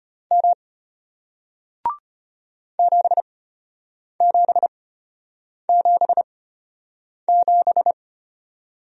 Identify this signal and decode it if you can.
Morse code, heard as a keyed tone, about 33 wpm, spaced as for 8 wpm, 700 Hz. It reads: M 7777